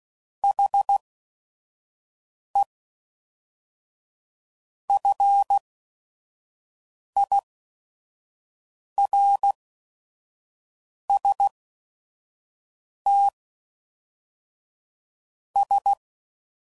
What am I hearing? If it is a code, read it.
Morse code, heard as a keyed tone, about 16 wpm, spaced as for 5 wpm, 800 Hz. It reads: HE FIRST S
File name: CW Code Groups @5WPM / Technical Subjects / SwPwrSup